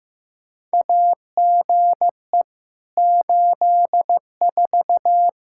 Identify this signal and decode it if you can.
Morse code, heard as a keyed tone, about 15 wpm, 700 Hz. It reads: AGE 84
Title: K1USN SST QSO Practice 15wpm